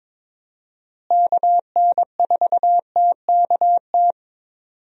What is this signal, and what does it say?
Morse code, heard as a keyed tone, about 22 wpm, 700 Hz. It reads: KN4TKT
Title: USA Call Signs - 6 Characters - Repeated 3x - 22wpm